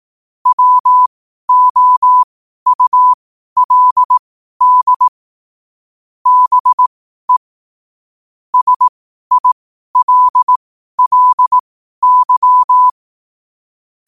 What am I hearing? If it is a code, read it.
Morse code, heard as a keyed tone, about 18 wpm, spaced as for 12 wpm, 1000 Hz. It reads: WOULD BE SILLY
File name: qotdfiles / 12wpm